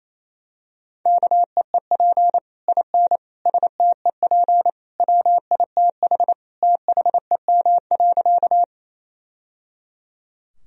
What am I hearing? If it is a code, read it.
Morse code, heard as a keyed tone, about 28 wpm, 700 Hz. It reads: KEEP IN STEP WITH THEM.
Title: Sentences from Top 1000 Words - Review Part 2 - 28wpm